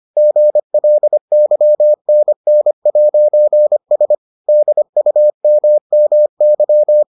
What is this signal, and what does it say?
Morse code, heard as a keyed tone, about 25 wpm, 600 Hz. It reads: GLYNN'S DUMMY